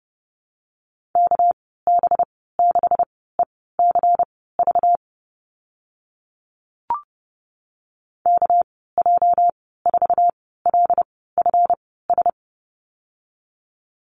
Morse code, heard as a keyed tone, about 30 wpm, spaced as for 17 wpm, 700 Hz. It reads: KB6ECV KJ4LFS
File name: Lesson 37g - Letter x - Call Signs 30wpm